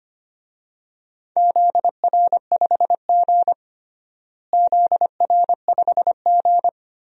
Morse code, heard as a keyed tone, about 25 wpm, 700 Hz. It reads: ZR5G ZR5G
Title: DXCC-100 Call Signs - 4 Characters - Repeated 3x - 25wpm